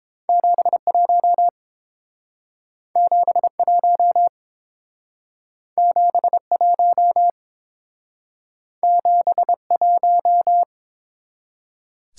Morse code, heard as a keyed tone, about 28 wpm, 700 Hz. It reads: 71 71 71 71